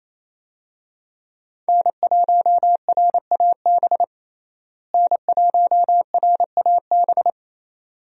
Morse code, heard as a keyed tone, about 28 wpm, 700 Hz. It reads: N1RAB N1RAB
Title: USA Call Signs - 5 Characters - Repeated 2x - 28wpm